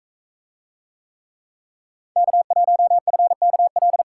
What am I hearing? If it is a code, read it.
Morse code, heard as a keyed tone, about 40 wpm, 700 Hz. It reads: K1FKL